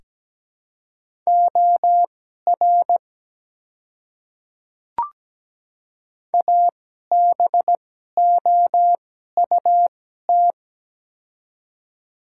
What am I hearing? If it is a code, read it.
Morse code, heard as a keyed tone, about 17 wpm, spaced as for 12 wpm, 700 Hz. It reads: OR ABOUT
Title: 2x Character Spacing - Top 100 Words - 17wpm